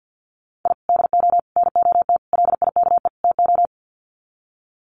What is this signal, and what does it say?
Morse code, heard as a keyed tone, about 50 wpm, 700 Hz. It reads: I DO NOT LIKE TO
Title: Sets of 5 Words 50wpm